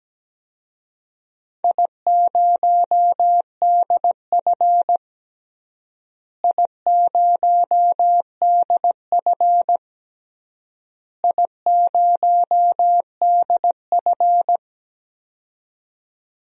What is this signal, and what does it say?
Morse code, heard as a keyed tone, about 17 wpm, 700 Hz. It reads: I0DF I0DF I0DF
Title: DXCC-100 Call Signs - 4 Characters - Repeated 3x - 17wpm